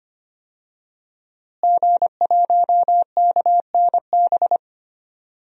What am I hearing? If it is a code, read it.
Morse code, heard as a keyed tone, about 25 wpm, 700 Hz. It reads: G1KNB
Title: DXCC-100 Call Signs - 5 Characters - Repeated 2x - 25wpm